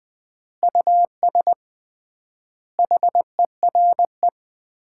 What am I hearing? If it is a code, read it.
Morse code, heard as a keyed tone, about 20 wpm, 700 Hz. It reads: US HERE
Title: 3x Word-Spacing - Sets of 5 Words - 20wpm